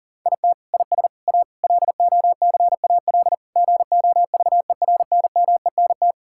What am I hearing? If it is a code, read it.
Morse code, heard as a keyed tone, about 40 wpm, 700 Hz. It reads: IT IS A LOCAL GOVERNMENT
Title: Sentences from Top 1000 Words - Review Part 5 - 40wpm